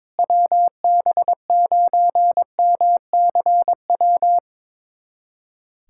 Morse code, heard as a keyed tone, about 22 wpm, 700 Hz. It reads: WB9MCW
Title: USA Callsign DE Callsign - 6 Characters - 22wpm